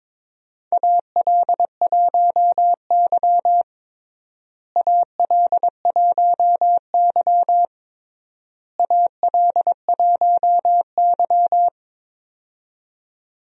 Morse code, heard as a keyed tone, about 22 wpm, 700 Hz. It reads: AL1Y AL1Y AL1Y
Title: USA Call Signs - 4 Characters - Repeated 3x - 22wpm